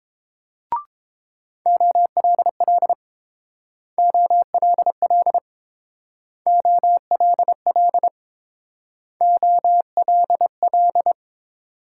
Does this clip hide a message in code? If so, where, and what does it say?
Yes: Morse code from 1.7 to 11.1 s, heard as a keyed tone, about 28 wpm, 700 Hz. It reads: OLL OLL OLL OLL